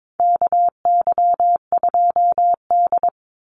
Morse code, heard as a keyed tone, about 22 wpm, 700 Hz. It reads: KY2D
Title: Call Signs - 22wpm